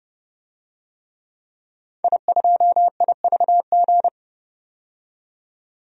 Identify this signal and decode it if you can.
Morse code, heard as a keyed tone, about 30 wpm, 700 Hz. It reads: I2IVG
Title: DXCC-100 Call Signs - 5 Characters - 30wpm